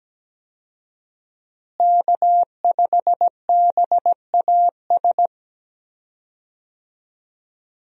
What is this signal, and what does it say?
Morse code, heard as a keyed tone, about 17 wpm, 700 Hz. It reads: K5BAS
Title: USA Call Signs - 5 Characters - Repeated 3x - 17wpm